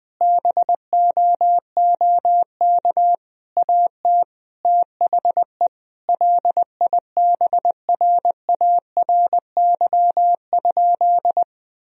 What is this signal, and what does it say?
Morse code, heard as a keyed tone, about 21 wpm, 700 Hz. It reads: BOOK AT THE LIBRARY?
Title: Sentences from Top 1000 Words - Review Part 4 - 20wpm